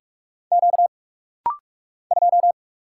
Morse code, heard as a keyed tone, about 44 wpm, 690 Hz. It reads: Q 2
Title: Single Letter-Number - Rapid-Fire 45wpm